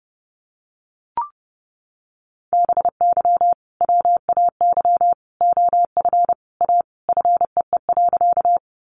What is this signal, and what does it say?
Morse code, heard as a keyed tone, about 30 wpm, 700 Hz. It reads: BY WAY OF A FEE.
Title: Sentences from Top 1500 Words 30wpm